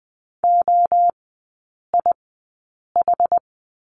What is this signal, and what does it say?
Morse code, heard as a keyed tone, about 20 wpm, spaced as for 8 wpm, 700 Hz. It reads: OIH
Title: 060 Lesson 11e - Letter h - Three Characters 20wpm